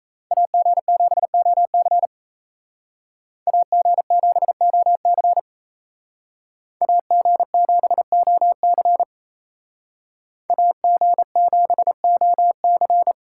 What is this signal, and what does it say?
Morse code, heard as a keyed tone, about 36 wpm, 700 Hz. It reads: AG7OC AG7OC AG7OC AG7OC